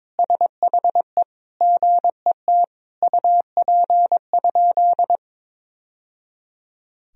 Morse code, heard as a keyed tone, about 21 wpm, 700 Hz. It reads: SHE GET UP?
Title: Sentences from Top 1000 Words - Review Part 1 - 22wpm